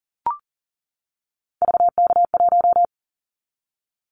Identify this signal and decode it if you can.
Morse code, heard as a keyed tone, about 40 wpm, 690 Hz. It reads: VK1
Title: Callsign Prefixes for Top 100 DXCC - 40wpm